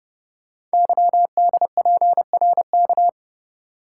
Morse code, heard as a keyed tone, about 30 wpm, 690 Hz. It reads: YDPRK
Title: Code Groups - 5 Letters - 30wpm